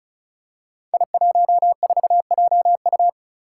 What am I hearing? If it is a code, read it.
Morse code, heard as a keyed tone, about 36 wpm, 700 Hz. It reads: I14JU
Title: Five Challenging Letter-Number Combinations 35wpm